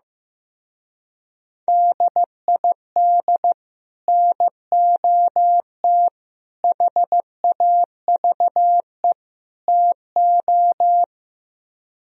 Morse code, heard as a keyed tone, about 15 wpm, 700 Hz. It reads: DID NOT HAVE TO
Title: Speed-Racing - Sets of 4 Words 15wpm